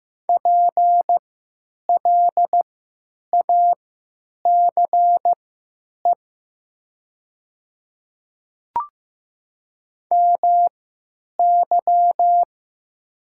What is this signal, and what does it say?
Morse code, heard as a keyed tone, about 15 wpm, spaced as for 8 wpm, 700 Hz. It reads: PLACE MY
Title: 3x Character Spacing - Top 200 Words - 15wpm